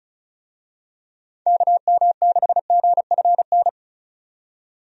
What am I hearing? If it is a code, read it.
Morse code, heard as a keyed tone, about 34 wpm, 700 Hz. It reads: KM6GFN
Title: USA Call Signs - 6 Characters - 35wpm